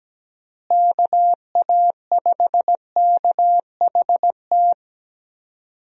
Morse code, heard as a keyed tone, about 17 wpm, 700 Hz. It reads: KA5KHT
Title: USA Call Signs - 6 Characters - Repeated 3x - 17wpm